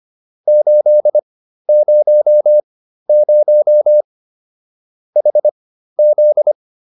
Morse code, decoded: 800 HZ

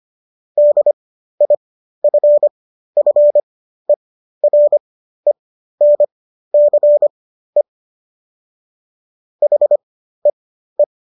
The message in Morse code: DIFFERENCE HEE

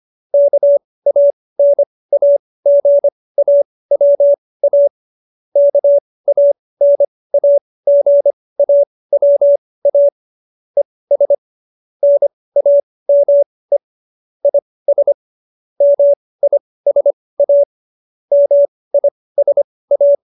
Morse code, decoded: KANAGAWA KANAGAWA ES NAME IS MISA MISA

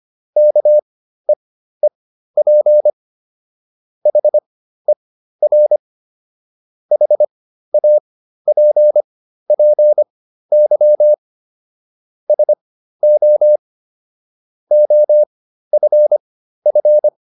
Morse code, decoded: KEEP HER HAPPY SO OFF

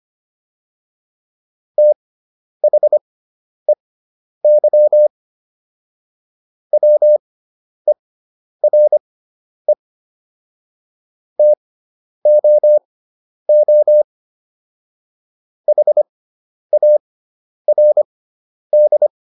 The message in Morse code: THEY WERE TOO HARD

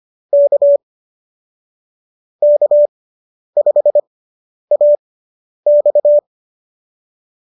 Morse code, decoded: K K5AX

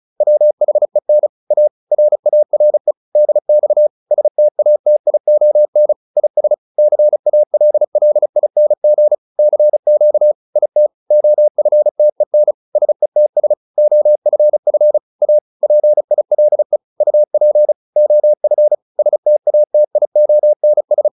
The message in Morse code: WHEN A RARE DX STATION IS CALLING CQ IT OFTEN SETS OFF A PILE UP OF STATIONS

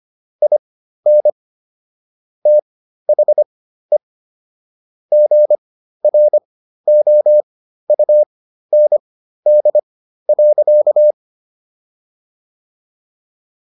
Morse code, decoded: IN THE GROUND.

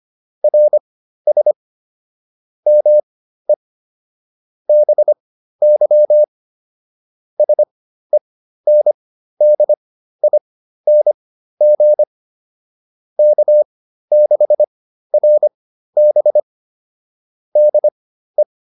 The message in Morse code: RS ME BY SENDING K6RB DE

